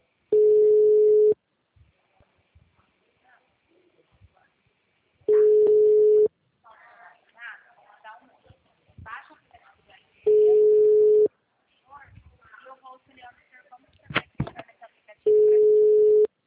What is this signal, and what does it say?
Ringback tone (Europe, 425 Hz), with speech.
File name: out- -2009- - - .69905